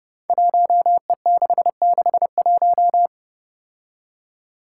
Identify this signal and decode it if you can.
Morse code, heard as a keyed tone, about 30 wpm, 700 Hz. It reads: 1E661
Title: Five Challenging Letter-Number Combinations 30wpm